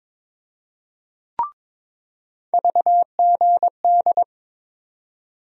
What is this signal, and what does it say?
Morse code, heard as a keyed tone, about 22 wpm, 700 Hz. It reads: VGD
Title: USA Call Sign Postfixes - 3 Letters 22wpm